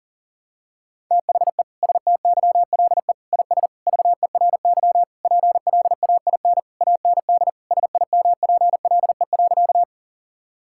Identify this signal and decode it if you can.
Morse code, heard as a keyed tone, about 40 wpm, 700 Hz. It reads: THE STYLE IS VERY PLAIN AND SIMPLE.
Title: Sentences from Top 1000 Words - Review Part 5 - 40wpm